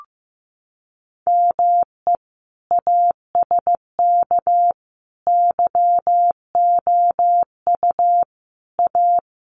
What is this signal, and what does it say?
Morse code, heard as a keyed tone, about 15 wpm, 700 Hz. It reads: ME ASK YOU A
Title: Sets of 5 Words Encore2 15wpm